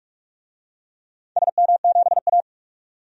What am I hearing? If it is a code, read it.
Morse code, heard as a keyed tone, about 46 wpm, 700 Hz. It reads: SM7A
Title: DXCC-100 Call Signs - 4 Characters - 45wpm